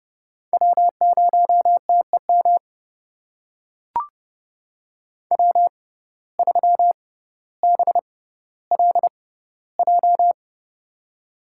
Morse code, heard as a keyed tone, about 30 wpm, spaced as for 15 wpm, 700 Hz. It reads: W0TEM W3BLJ
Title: USA Call Signs - 5 Characters - 6x Character Spacing - 30wpm